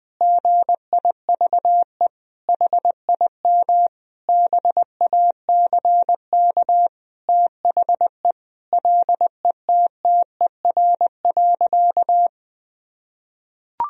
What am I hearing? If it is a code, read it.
Morse code, heard as a keyed tone, about 20 wpm, 700 Hz. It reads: GIVE HIM BACK THE LETTER.
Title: Sentences from Top 1000 Words - Review Part 4 - 20wpm